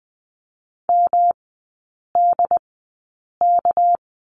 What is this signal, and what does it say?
Morse code, heard as a keyed tone, about 19 wpm, spaced as for 8 wpm, 690 Hz. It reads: MDK
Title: 205 Lesson 33e - Letter j - Three Characters 20wpm